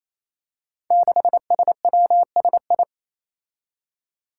Morse code, heard as a keyed tone, about 28 wpm, 700 Hz. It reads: 6SWSI
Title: Five Challenging Letter-Number Combinations 28wpm